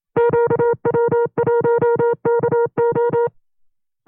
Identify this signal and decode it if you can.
Morse code, heard as a keyed tone, about 27 wpm, 470 Hz. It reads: QW1KO
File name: FreeFiles / CW-practice / 25WPM-side1